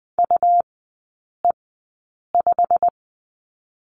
Morse code, heard as a keyed tone, about 20 wpm, spaced as for 8 wpm, 700 Hz. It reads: UE5
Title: 107 Lesson 18e - Letter m - Three Characters 20wpm